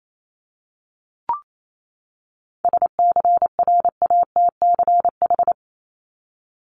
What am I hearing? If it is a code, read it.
Morse code, heard as a keyed tone, about 28 wpm, 700 Hz. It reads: SCRATCH
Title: Top 1501-1750 Words 28wpm